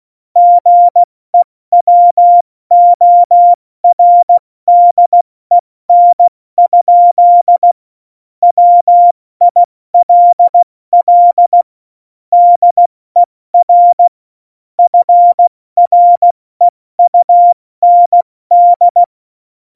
Morse code, heard as a keyed tone, about 16 wpm, 700 Hz. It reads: GEWORDEN? WILL DER FREUND
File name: Morsekurs / Witze / Witz 08. Januar 2019, 70 F80